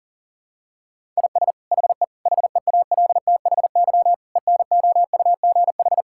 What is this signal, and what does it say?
Morse code, heard as a keyed tone, about 40 wpm, 700 Hz. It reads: IS HE HEALTHY ENOUGH
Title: Sentences from Top 1000 Words - Review Part 5 - 40wpm